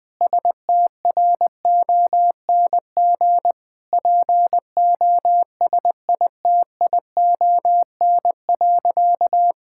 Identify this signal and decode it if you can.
Morse code, heard as a keyed tone, about 20 wpm, 700 Hz. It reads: STRONG POSITION.
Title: Sentences from Top 1000 Words - Review Part 5 - 20wpm